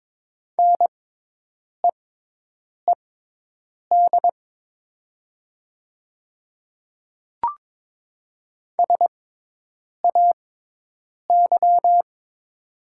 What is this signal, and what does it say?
Morse code, heard as a keyed tone, about 22 wpm, spaced as for 8 wpm, 700 Hz. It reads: NEED SAY
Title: 6x Character Spacing - Top 300 Words - 22wpm